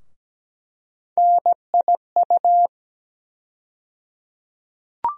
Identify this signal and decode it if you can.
Morse code, heard as a keyed tone, about 17 wpm, 700 Hz. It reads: NIU
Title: USA Call Sign Postfixes - 3 Letters 17wpm